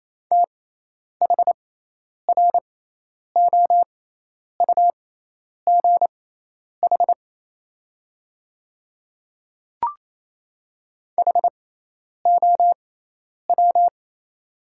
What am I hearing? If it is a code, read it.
Morse code, heard as a keyed tone, about 28 wpm, spaced as for 10 wpm, 700 Hz. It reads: THROUGH HOW
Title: 6x Character Spacing - Top 500 Words - 28wpm